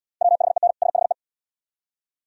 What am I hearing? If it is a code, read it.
Morse code, heard as a keyed tone, about 75 wpm, 700 Hz. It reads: WHEN SHE